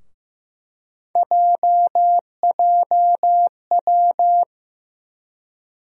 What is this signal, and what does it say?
Morse code, heard as a keyed tone, about 15 wpm, 700 Hz. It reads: JJW
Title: USA Call Sign Postfixes - 3 Letters 15wpm